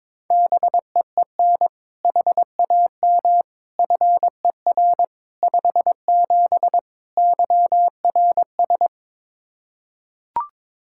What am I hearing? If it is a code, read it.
Morse code, heard as a keyed tone, about 22 wpm, 700 Hz. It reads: BEEN HAM FER 57 YRS
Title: K1USN SST QSO Practice 22wpm